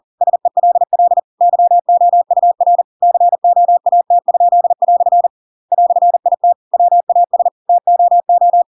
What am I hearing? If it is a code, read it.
Morse code, heard as a keyed tone, about 40 wpm, 700 Hz. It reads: SELL YOUR COAT?" "IT WAS TOO
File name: Pinocchio / Pinocchio ch8 40 40 ik1hge